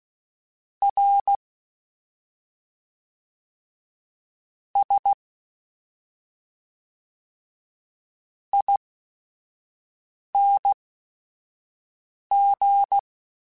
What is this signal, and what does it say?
Morse code, heard as a keyed tone, about 16 wpm, spaced as for 5 wpm, 800 Hz. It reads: R S ING
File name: Special INGs / Special INGs-mp3s / SlowToFaster ing @5